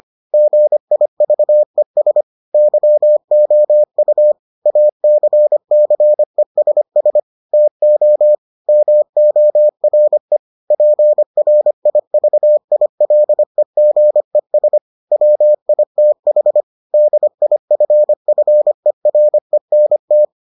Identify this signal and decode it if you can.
Morse code, heard as a keyed tone, about 25 wpm, 600 Hz. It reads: GIVES YOU ACCESS TO MORE PRIVILEGES WITH DIFFERENT